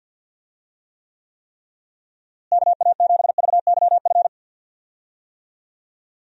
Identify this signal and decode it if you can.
Morse code, heard as a keyed tone, about 50 wpm, 700 Hz. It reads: KA7VYF
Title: USA Call Signs - 6 Characters - Repeated 3x - 50wpm